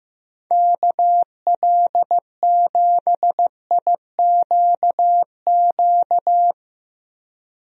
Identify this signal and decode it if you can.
Morse code, heard as a keyed tone, about 15 wpm, 700 Hz. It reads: KL7IQQ
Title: DXCC-100 Call Signs - 6 Characters - 15wpm